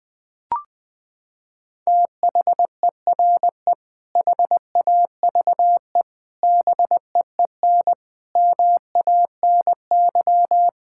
Morse code, heard as a keyed tone, about 20 wpm, 700 Hz. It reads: THERE HAVE BEEN MANY
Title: Sentences with 3-Word Phrases - 20wpm